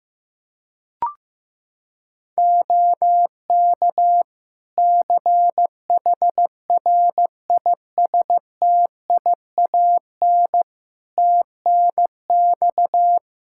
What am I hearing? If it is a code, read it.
Morse code, heard as a keyed tone, about 15 wpm, 700 Hz. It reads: OK CHRISTIAN TNX
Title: Practice QSO - Exchange 4 - Ending - 15wpm